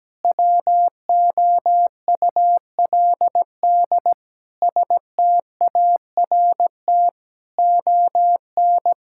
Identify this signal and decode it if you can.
Morse code, heard as a keyed tone, about 17 wpm, 700 Hz. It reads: WOULD START ON
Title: Sentences from Top 1600 Words 17wpm